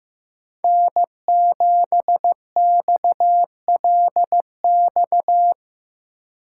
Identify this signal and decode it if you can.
Morse code, heard as a keyed tone, about 16 wpm, 700 Hz. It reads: N7XLX
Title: USA Call Signs - 5 Characters - 4x Character Spacing - 15wpm